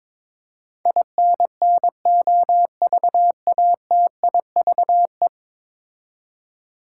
Morse code, heard as a keyed tone, about 22 wpm, 700 Hz. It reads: INNOVATIVE
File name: practice / 22wpm / 10 Letter Words - First 3 Letters Given - 22wpm